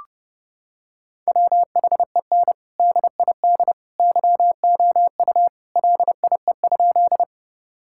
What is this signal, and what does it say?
Morse code, heard as a keyed tone, about 30 wpm, 700 Hz. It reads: WHEN DID YOU LIE?
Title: Sentences from Top 1600 Words 30wpm